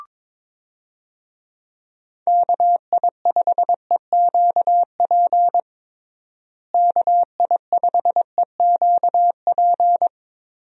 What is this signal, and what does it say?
Morse code, heard as a keyed tone, about 21 wpm, 700 Hz. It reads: KI5EQP KI5EQP
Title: USA Call Signs - 6 Characters - Repeated 3x - 22wpm